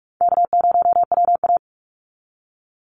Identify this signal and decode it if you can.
Morse code, heard as a keyed tone, about 44 wpm, 700 Hz. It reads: K0WA
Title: Easy Call Signs Repeated 2x - 45wpm